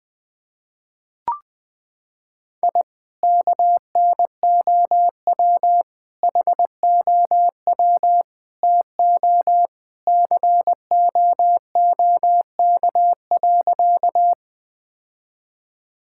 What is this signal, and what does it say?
Morse code, heard as a keyed tone, about 20 wpm, 700 Hz. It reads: I KNOW HOW TO COOK.
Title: Sentences with 3-Word Phrases - 20wpm